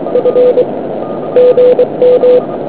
Morse code, heard as a keyed tone, about 24 wpm, 500 Hz, with noise. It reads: F GM